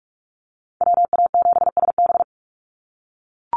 Morse code, heard as a keyed tone, about 44 wpm, 700 Hz. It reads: WA7SB
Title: NA SOTA Chaser Callsigns 45wpm